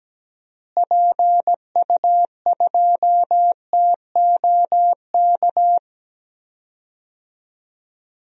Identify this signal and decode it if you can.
Morse code, heard as a keyed tone, about 17 wpm, 700 Hz. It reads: PU2TOK
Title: DXCC-100 Call Signs - 6 Characters - Repeated 3x - 17wpm